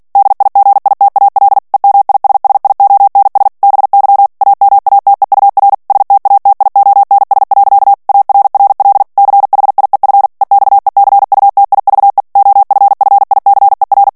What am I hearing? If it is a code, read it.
Morse code, heard as a keyed tone, about 48 wpm, 800 Hz. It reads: DIGITAL EMISSIONS BY AMATEUR STATIONS. ARRL CHIEF EXECUTIVE OFFICER
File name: ANGLAIS texte clair 3 vitesse 45 mots minute